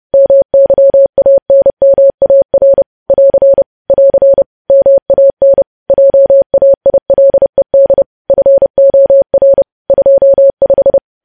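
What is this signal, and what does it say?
Morse code, heard as a keyed tone, about 30 wpm, 550 Hz. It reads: MYANMAR <AR> <AR> MAN JAILED FOR 25
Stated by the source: Morse code news headlines sent at 30 words per minute.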